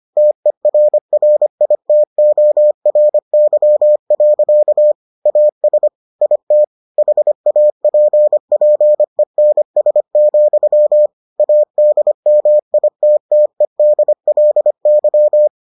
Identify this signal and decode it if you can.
Morse code, heard as a keyed tone, about 25 wpm, 600 Hz. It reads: TERRITORY. AS IT HAPPENS, ADMITTEDLY